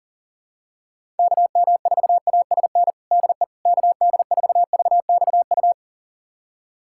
Morse code, heard as a keyed tone, about 37 wpm, 700 Hz. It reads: KM4ASN DE KD4VXU
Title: USA Callsign DE Callsign - 6 Characters - 40wpm